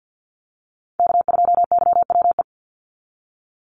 Morse code, heard as a keyed tone, about 54 wpm, 700 Hz. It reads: K2YWE
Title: Call Signs - 50wpm